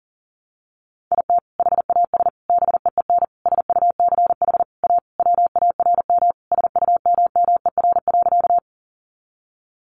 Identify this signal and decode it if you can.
Morse code, heard as a keyed tone, about 40 wpm, 700 Hz. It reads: IT HAS BEEN SUCH A WARM SUMMER.
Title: Sentences from Top 1100 Words 40wpm